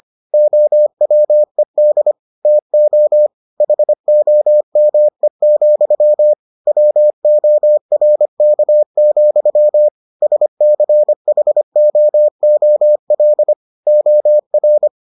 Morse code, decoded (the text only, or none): OWED TO HOME, WORK, SCHOOL OR